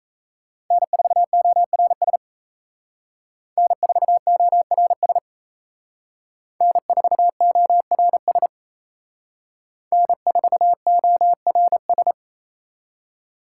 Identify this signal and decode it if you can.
Morse code, heard as a keyed tone, about 36 wpm, 700 Hz. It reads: N4ORS N4ORS N4ORS N4ORS